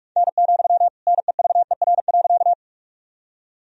Morse code, heard as a keyed tone, about 46 wpm, 700 Hz. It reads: N, NEVER.
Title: Sentences from Top 300 Words and Abbreviations 45wpm